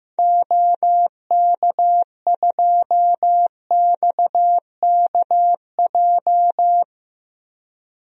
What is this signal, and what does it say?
Morse code, heard as a keyed tone, about 15 wpm, 700 Hz. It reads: OK2XKJ
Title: DXCC-100 Call Signs - 6 Characters - 15wpm